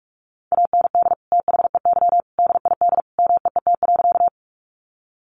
Morse code, heard as a keyed tone, about 44 wpm, 700 Hz. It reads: AND THEY DID MEET.
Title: Sentences from Top 1100 Words 45wpm